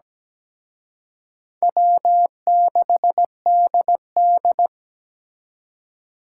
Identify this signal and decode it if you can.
Morse code, heard as a keyed tone, about 20 wpm, 700 Hz. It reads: W6DD
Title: Speed Racing - USA 1x2 Call Signs 17wpm